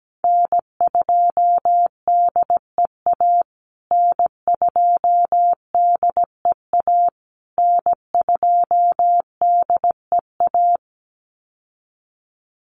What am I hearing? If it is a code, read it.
Morse code, heard as a keyed tone, about 17 wpm, 700 Hz. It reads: N2DEA N2DEA N2DEA
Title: USA Call Signs Repeated 3x 17wpm